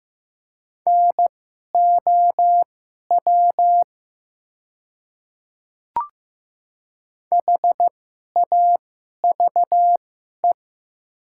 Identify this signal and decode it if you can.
Morse code, heard as a keyed tone, about 15 wpm, spaced as for 11 wpm, 700 Hz. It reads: NOW HAVE
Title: 2x Character Spacing - Top 100 Words - 15wpm